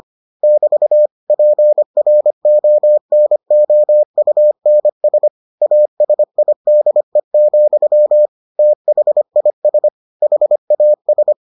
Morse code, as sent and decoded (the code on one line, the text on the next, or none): -...- / .--. .-. --- -. --- ..- -. ... / .- ... .. -.. --..-- / - .... .. ... / .... .- ...
<BT> PRONOUNS ASIDE, THIS HAS